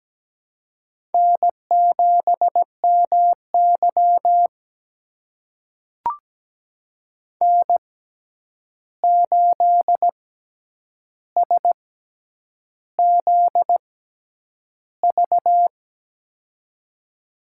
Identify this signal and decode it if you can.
Morse code, heard as a keyed tone, about 17 wpm, spaced as for 8 wpm, 700 Hz. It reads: N7MY N8SZV
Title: USA Call Signs - 6x Character Spacing - 17wpm